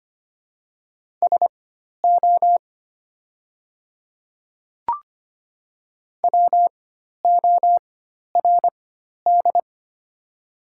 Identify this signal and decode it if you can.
Morse code, heard as a keyed tone, about 25 wpm, spaced as for 12 wpm, 700 Hz. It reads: SO WORD
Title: 4x Character Spacing - Top 100 Words - 25wpm